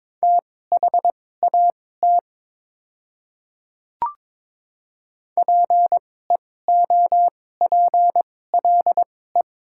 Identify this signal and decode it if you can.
Morse code, heard as a keyed tone, about 22 wpm, spaced as for 16 wpm, 700 Hz. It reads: THAT PEOPLE
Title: 2x Character Spacing - Top 200 Words - 22wpm